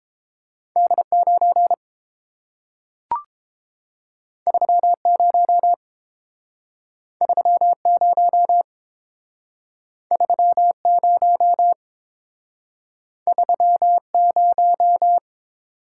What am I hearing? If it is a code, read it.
Morse code, heard as a keyed tone, about 30 wpm, 700 Hz. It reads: D9 30 30 30 30